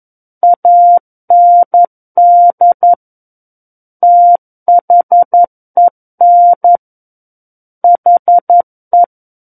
Morse code, heard as a keyed tone, about 11 wpm, 700 Hz. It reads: AND THEN HE
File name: 11 WPM